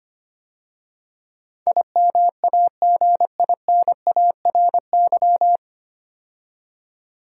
Morse code, heard as a keyed tone, about 25 wpm, 700 Hz. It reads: IMAGINARY
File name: practice / 25wpm / 9 Letter Words - First 3 Letters Given - 25wpm